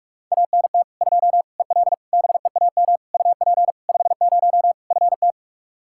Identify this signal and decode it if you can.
Morse code, heard as a keyed tone, about 46 wpm, 700 Hz. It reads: ANT 2 EL BEAM UP 50 FT
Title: Practice QSO - Exchange 2 - Info - 45wpm